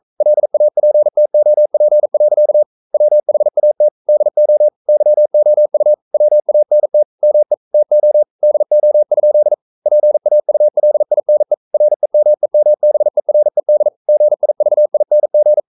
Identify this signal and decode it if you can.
Morse code, heard as a keyed tone, about 42 wpm, 600 Hz. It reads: LAPTOP. WHAT DO YOU WANT ME TO DO? PAULINE REMEMBERED GIVING
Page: Romanen Never af Ken Follet på engelsk.